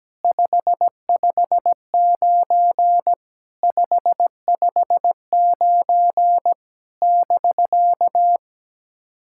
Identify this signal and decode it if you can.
Morse code, heard as a keyed tone, about 17 wpm, 700 Hz. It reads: 559 559 <BK>
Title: SOTA QSO Practice 17wpm